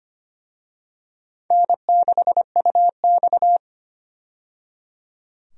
Morse code, heard as a keyed tone, about 25 wpm, 700 Hz. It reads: N6UX
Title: USA Call Signs - 4 Characters - 25wpm